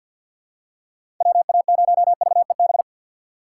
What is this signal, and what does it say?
Morse code, heard as a keyed tone, about 47 wpm, 700 Hz. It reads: WA0VEB
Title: USA Call Signs - 6 Characters - 4x Character Spacing - 50wpm